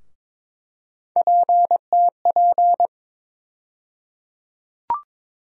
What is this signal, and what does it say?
Morse code, heard as a keyed tone, about 22 wpm, 700 Hz. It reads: PTP
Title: USA Call Sign Postfixes - 3 Letters 22wpm